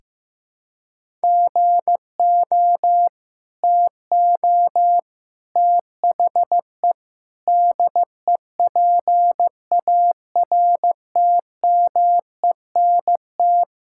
Morse code, heard as a keyed tone, about 15 wpm, 700 Hz. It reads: GO TO THE DEPARTMENT
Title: Sentences from Top 1000 Words - Review Part 5 - 15wpm